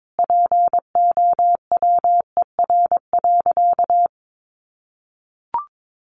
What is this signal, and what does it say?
Morse code, heard as a keyed tone, about 22 wpm, 700 Hz. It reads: POWER.
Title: Sentences from Top 600 Words 22wpm